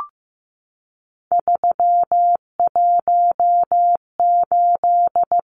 Morse code, heard as a keyed tone, about 15 wpm, 700 Hz. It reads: 318
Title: Numbers - 3-Digit 15wpm